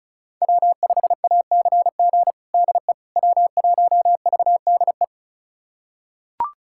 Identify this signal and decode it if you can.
Morse code, heard as a keyed tone, about 36 wpm, 700 Hz. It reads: W5ACG DE W1VDE
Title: USA Callsign DE Callsign - 5 Characters - 35wpm